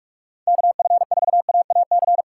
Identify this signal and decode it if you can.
Morse code, heard as a keyed tone, about 47 wpm, 700 Hz. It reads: KF4AAC
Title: USA Callsign DE Callsign 45wpm